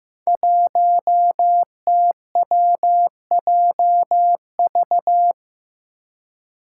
Morse code, heard as a keyed tone, about 15 wpm, 700 Hz. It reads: 1TWJV
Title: Five Challenging Letter-Number Combinations 15wpm